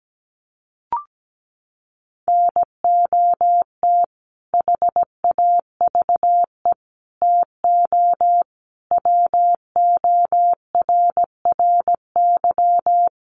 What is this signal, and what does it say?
Morse code, heard as a keyed tone, about 17 wpm, 700 Hz. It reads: NOT HAVE TO WORRY
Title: Sets of 5 Words 17wpm